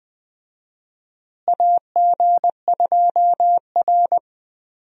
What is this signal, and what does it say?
Morse code, heard as a keyed tone, about 20 wpm, 700 Hz. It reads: AG2R
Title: USA Call Signs - 4 Characters - Repeated 3x - 20wpm